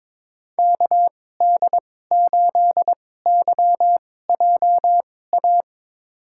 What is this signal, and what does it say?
Morse code, heard as a keyed tone, about 21 wpm, spaced as for 16 wpm, 700 Hz. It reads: KD8YJA
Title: USA Call Signs - 6 Characters - 2x Character Spacing - 22wpm